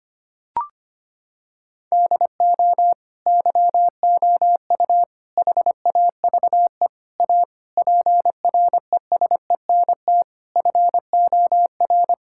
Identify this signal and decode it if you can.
Morse code, heard as a keyed tone, about 25 wpm, 700 Hz. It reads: DO YOU HAVE A PRESENT FOR
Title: Sentences from Top 1000 Words - Review Part 5 - 25wpm